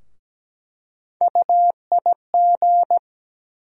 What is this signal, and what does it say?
Morse code, heard as a keyed tone, about 17 wpm, 700 Hz. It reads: UIG
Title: USA Call Sign Postfixes - 3 Letters 17wpm